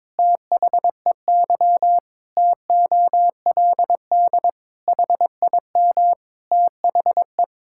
Morse code, heard as a keyed tone, about 22 wpm, 700 Hz. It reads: THEY TOLD HIM THE
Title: Pangram Challenge v2 - 22wpm